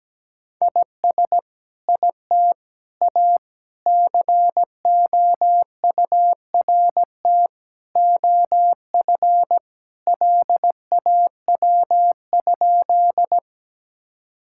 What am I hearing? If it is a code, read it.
Morse code, heard as a keyed tone, about 17 wpm, 700 Hz. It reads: IS IT A COURT OF LAW?
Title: Sentences from Top 1600 Words 17wpm